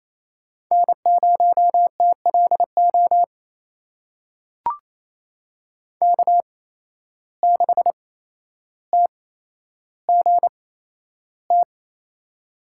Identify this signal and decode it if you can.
Morse code, heard as a keyed tone, about 28 wpm, spaced as for 12 wpm, 700 Hz. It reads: N0TLO K6TGT
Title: USA Call Signs - 5 Characters - 8x Character Spacing - 28wpm